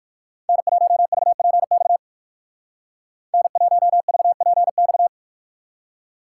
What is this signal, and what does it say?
Morse code, heard as a keyed tone, about 58 wpm, 700 Hz. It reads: N1VPX N1VPX